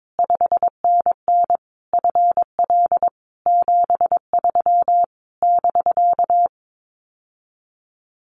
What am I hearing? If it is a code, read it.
Morse code, heard as a keyed tone, about 22 wpm, 700 Hz. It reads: 5NN FL 73 <BK>
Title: Common QSO Phrases Encore 22wpm